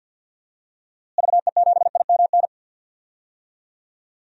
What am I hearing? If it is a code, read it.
Morse code, heard as a keyed tone, about 51 wpm, 700 Hz. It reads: VE7IMN
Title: DXCC-100 Call Signs - 6 Characters - Repeated 2x - 50wpm